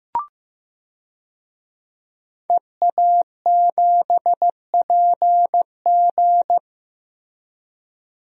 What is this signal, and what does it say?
Morse code, heard as a keyed tone, about 15 wpm, 700 Hz. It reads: EA7PG
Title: DXCC-100 Call Signs - 5 Characters - Repeated 3x - 15wpm